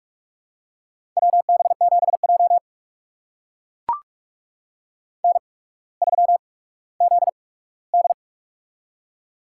Morse code, heard as a keyed tone, about 44 wpm, spaced as for 19 wpm, 700 Hz. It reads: WB7J N3ZD
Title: USA Call Signs - 4 Characters - 8x Character Spacing - 45wpm